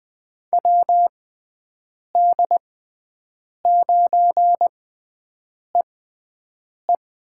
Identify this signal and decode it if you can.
Morse code, heard as a keyed tone, about 21 wpm, spaced as for 7 wpm, 700 Hz. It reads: WD9EE
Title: USA Call Signs - 6 Characters - 6x Character Spacing - 20wpm